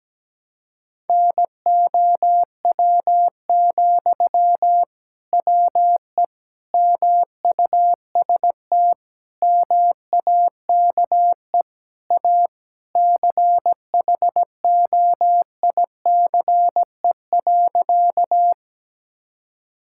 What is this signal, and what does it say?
Morse code, heard as a keyed tone, about 17 wpm, 700 Hz. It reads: NOW, WE MUST MAKE A CHOICE.
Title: Sentences from Top 1000 Words - Review Part 6 - 17wpm